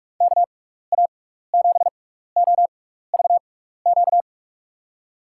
Morse code, heard as a keyed tone, about 47 wpm, spaced as for 16 wpm, 700 Hz. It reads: KA7OVQ
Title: USA Call Signs - 6 Characters - 6x Character Spacing - 45wpm